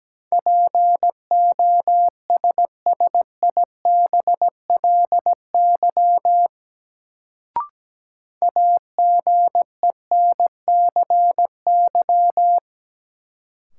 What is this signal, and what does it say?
Morse code, heard as a keyed tone, about 17 wpm, 700 Hz. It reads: POSSIBLY AGENCY
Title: Sentences from Top 1300 Words - New Word Drill Alt - 17wpm